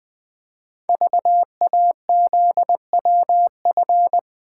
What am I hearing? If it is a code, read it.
Morse code, heard as a keyed tone, about 20 wpm, 700 Hz. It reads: VAZWF
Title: Code Groups - 5 Letter:Numbers - 20wpm